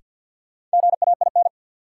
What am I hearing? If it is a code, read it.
Morse code, heard as a keyed tone, about 53 wpm, 700 Hz. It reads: GAIN
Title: Sentences from Top 1000 Words - New Word Drill - 50wpm